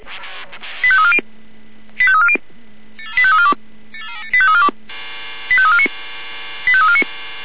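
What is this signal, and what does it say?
Five-tone selective call (ZVEI), heard as quick runs of five tones, tones of about 70 ms. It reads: 85340 85340 85342 85349 85349
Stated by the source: ZVEI five-tone call received by SDR.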